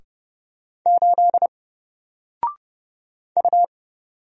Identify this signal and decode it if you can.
Morse code, heard as a keyed tone, about 31 wpm, 700 Hz. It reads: 8 U
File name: Single-Letter-Number-30wpm